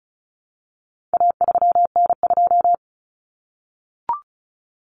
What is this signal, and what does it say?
Morse code, heard as a keyed tone, about 34 wpm, 700 Hz. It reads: A3N2
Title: Common Cut Numbers - 4-Digit 35wpm